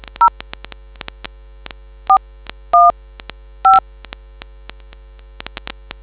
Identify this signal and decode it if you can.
DTMF, heard as telephone touch tones, tones of about 130 ms. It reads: 0 415